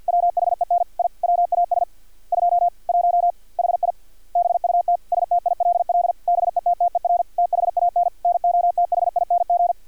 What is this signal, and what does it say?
Morse code, heard as a keyed tone, about 50 wpm, 700 Hz. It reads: WHEN I WAS 21 HI BUT STILL BETTER THAN NOTHING
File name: archived / CRSnet